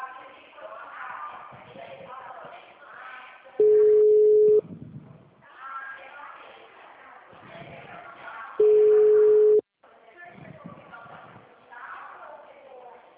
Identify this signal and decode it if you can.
Ringback tone (Europe, 425 Hz), with speech.